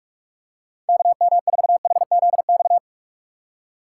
Morse code, heard as a keyed tone, about 42 wpm, 700 Hz. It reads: KM4HZX
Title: USA Call Signs - 6 Characters - 2x Character Spacing - 45wpm